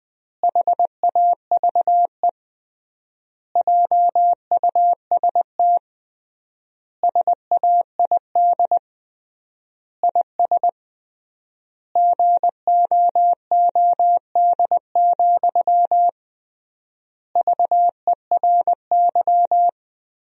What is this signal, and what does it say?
Morse code, heard as a keyed tone, about 20 wpm, 700 Hz. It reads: HAVE JUST SAID IS GOOD, VERY
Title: 3x Word-Spacing - Sentences from Top 200 Words - 20wpm